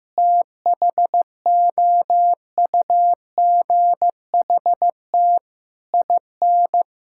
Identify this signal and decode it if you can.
Morse code, heard as a keyed tone, about 15 wpm, 700 Hz. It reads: THOUGHT IN
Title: Sentences from Top 1000 Words - Review Part 6 - 15wpm